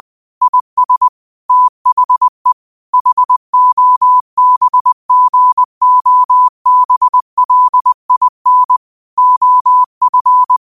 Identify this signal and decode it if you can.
Morse code, heard as a keyed tone, about 20 wpm, 1000 Hz. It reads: IS THE HOBGOBLIN OF
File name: qotdfiles / 20wpm